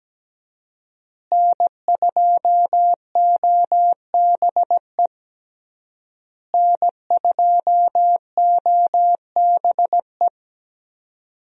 Morse code, heard as a keyed tone, about 17 wpm, 700 Hz. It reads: N2OBE N2OBE
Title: USA Call Signs - 5 Characters - Repeated 3x - 17wpm